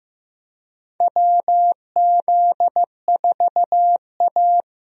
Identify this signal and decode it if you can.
Morse code, heard as a keyed tone, about 15 wpm, 700 Hz. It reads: WZ4A
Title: USA Call Signs - 4 Characters - Repeated 3x - 15wpm